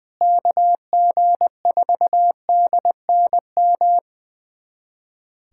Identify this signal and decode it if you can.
Morse code, heard as a keyed tone, about 20 wpm, 700 Hz. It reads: KG4DNM
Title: USA Callsign DE Callsign 20wpm